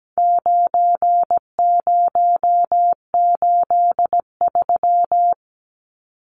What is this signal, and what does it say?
Morse code, heard as a keyed tone, about 17 wpm, 700 Hz. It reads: 9083
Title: Numbers - 4-Digit 17wpm